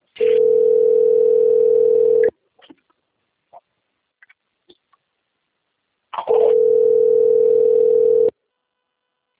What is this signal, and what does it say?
Ringback tone (North America), 440 + 480 Hz.
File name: rg-1600-unknown- - - .36035